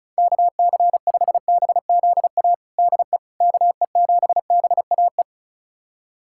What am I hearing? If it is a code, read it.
Morse code, heard as a keyed tone, about 36 wpm, 700 Hz. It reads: KC5BZA DE KE7BAE
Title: USA Callsign DE Callsign 35wpm